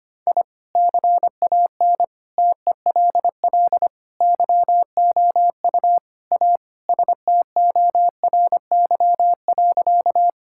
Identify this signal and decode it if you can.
Morse code, heard as a keyed tone, about 25 wpm, 700 Hz. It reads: I CAN TELL YOU A STORY.
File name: practice / 25wpm / Sentences with 4-Word Phrases - 25wpm